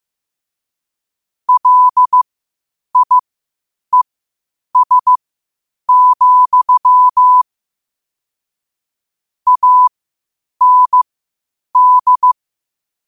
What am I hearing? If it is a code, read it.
Morse code, heard as a keyed tone, about 15 wpm, spaced as for 8 wpm, 1000 Hz. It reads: LIES, AND
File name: qotdfiles / 8wpm